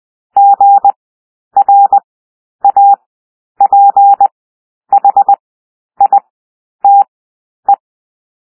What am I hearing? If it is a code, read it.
Morse code, heard as a keyed tone, about 20 wpm, spaced as for 10 wpm, 800 Hz. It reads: GRAPHITE
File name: cwmp3 / W1 SOUND WORDS 20 wpm